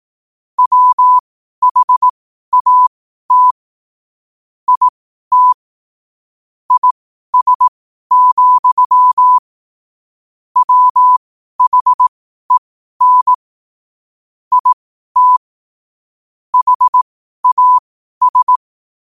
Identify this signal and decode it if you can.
Morse code, heard as a keyed tone, about 18 wpm, spaced as for 12 wpm, 1000 Hz. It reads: WHAT IT IS, WHEN IT HAS
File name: qotdfiles / 12wpm